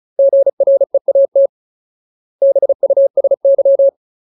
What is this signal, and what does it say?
Morse code, heard as a keyed tone, about 34 wpm, 550 Hz. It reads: GREAT BUSY
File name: Ultimate Word List / 35WPM Ultimate